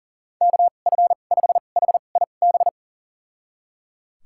Morse code, heard as a keyed tone, about 37 wpm, spaced as for 28 wpm, 700 Hz. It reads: KF5HIB
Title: USA Call Signs - 2x Character Spacing - 40wpm